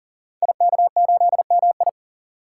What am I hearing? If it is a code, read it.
Morse code, heard as a keyed tone, about 42 wpm, 690 Hz. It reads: IK8MI
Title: Code Groups - 5 Letter:Numbers - 40wpm